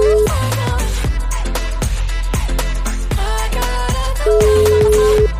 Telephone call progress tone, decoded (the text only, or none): ringback tone